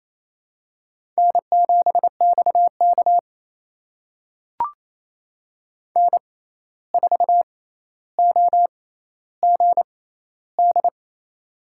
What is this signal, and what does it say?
Morse code, heard as a keyed tone, about 28 wpm, spaced as for 13 wpm, 700 Hz. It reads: N7XK N4OGD
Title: USA Call Signs - 6x Character Spacing - 28wpm